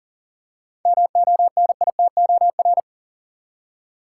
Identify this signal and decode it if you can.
Morse code, heard as a keyed tone, about 40 wpm, 700 Hz. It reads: MONITOR
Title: Seven Letter Words - 40wpm